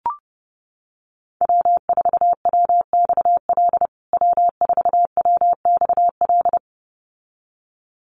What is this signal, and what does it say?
Morse code, heard as a keyed tone, about 30 wpm, 700 Hz. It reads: W4WXL W4WXL
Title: USA Call Signs Repeated 2x 30wpm